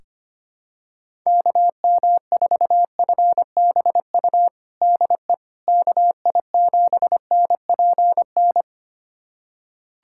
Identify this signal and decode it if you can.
Morse code, heard as a keyed tone, about 25 wpm, 700 Hz. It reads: KM4FBU DE KI7NPN
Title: USA Callsign DE Callsign 25wpm